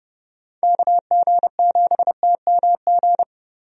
Morse code, heard as a keyed tone, about 31 wpm, 700 Hz. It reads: KG7TMG
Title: USA Call Signs - 6 Characters - 30wpm